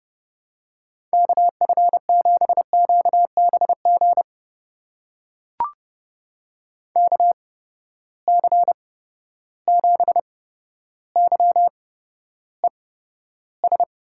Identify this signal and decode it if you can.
Morse code, heard as a keyed tone, about 30 wpm, spaced as for 13 wpm, 700 Hz. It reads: KF7QBG KC7YES
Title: USA Call Signs - 6 Characters - 8x Character Spacing - 30wpm